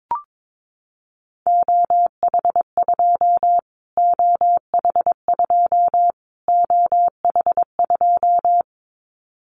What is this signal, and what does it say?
Morse code, heard as a keyed tone, about 22 wpm, 700 Hz. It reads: OH2 OH2 OH2
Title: Callsign Prefixes for Top 25 DXCC - Repeated 3x - 22wpm